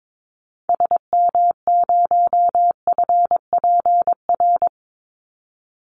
Morse code, heard as a keyed tone, about 22 wpm, 700 Hz. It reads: SM0FPR
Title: Call Signs Repeated 3x - 22wpm